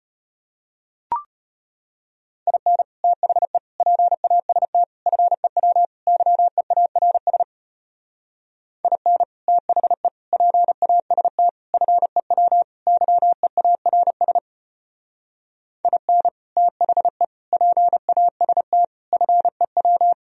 Morse code, heard as a keyed tone, about 31 wpm, 700 Hz. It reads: IN THE PAST FEW YEARS IN THE PAST FEW YEARS IN THE PAST FEW
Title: Speed-Racing - Sets of 5 Words 25wpm